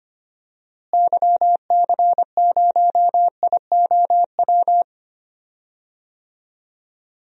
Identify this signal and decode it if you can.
Morse code, heard as a keyed tone, about 25 wpm, 700 Hz. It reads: YC0IOW
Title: DXCC-100 Call Signs - 6 Characters - Repeated 3x - 25wpm